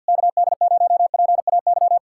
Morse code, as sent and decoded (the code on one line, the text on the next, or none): -.- -.. ----- .--. .- -.--
KD0PAY